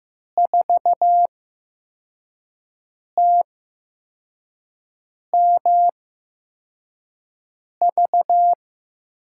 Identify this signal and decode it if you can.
Morse code, heard as a keyed tone, about 15 wpm, spaced as for 4 wpm, 700 Hz. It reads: 4TMV
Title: USA Call Signs - 5 Characters - 8x Character Spacing - 15wpm